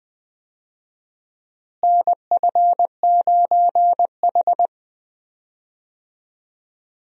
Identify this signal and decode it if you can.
Morse code, heard as a keyed tone, about 20 wpm, 700 Hz. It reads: NF9H
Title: USA Call Signs - 4 Characters - Repeated 2x - 20wpm